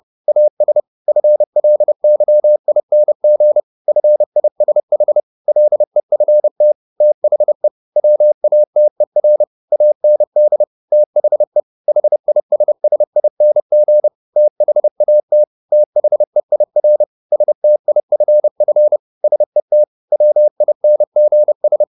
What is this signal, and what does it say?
Morse code, heard as a keyed tone, about 30 wpm, 600 Hz. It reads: AS FLYING FISH LEFT THE WATER AND THE HISSING THAT THEIR STIFF SET WINGS
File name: the old man and the sea / the old man and the sea 1 30wpm